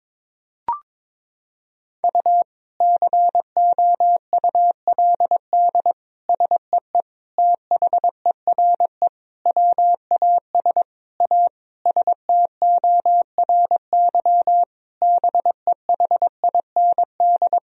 Morse code, heard as a keyed tone, about 22 wpm, 700 Hz. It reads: U COULD SEE THERE WAS A STORY BEHIND
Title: Sentences from Top 500 Words and Abbreviations 22wpm